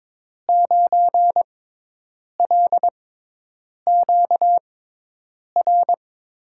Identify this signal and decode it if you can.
Morse code, heard as a keyed tone, about 22 wpm, spaced as for 8 wpm, 700 Hz. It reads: 9LQR
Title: USA Call Signs - 5 Characters - 6x Character Spacing - 22wpm